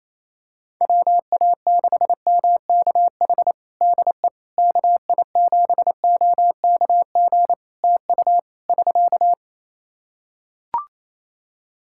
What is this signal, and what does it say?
Morse code, heard as a keyed tone, about 28 wpm, 700 Hz. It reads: WA6MKH DE KI7OKG TU <SK>
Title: Practice QSO - Exchange 4 - Ending - 28wpm